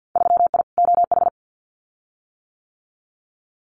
Morse code, heard as a keyed tone, about 50 wpm, 700 Hz. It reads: 3I OH
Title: Winter Field Day Exchanges Encore 50wpm